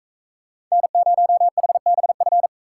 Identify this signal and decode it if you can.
Morse code, heard as a keyed tone, about 41 wpm, 700 Hz. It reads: N0HBF